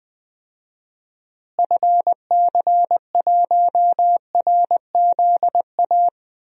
Morse code, heard as a keyed tone, about 20 wpm, 700 Hz. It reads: FC1RZA
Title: DXCC-100 Call Signs - 6 Characters - Repeated 2x - 20wpm